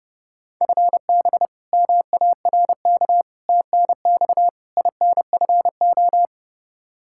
Morse code, heard as a keyed tone, about 32 wpm, 700 Hz. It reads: FB MARK TNX INFO
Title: K1USN SST QSO Practice 30wpm